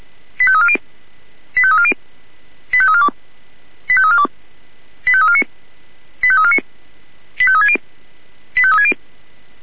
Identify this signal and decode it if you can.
Five-tone selective call (ZVEI), heard as quick runs of five tones, tones of about 70 ms. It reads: 85340 85340 85342 85342 85349 85349 85360 85360